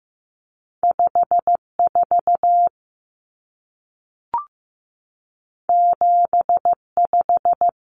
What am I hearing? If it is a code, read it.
Morse code, heard as a keyed tone, about 15 wpm, 700 Hz. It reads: 54 75
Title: Numbers - 2-Digit 15wpm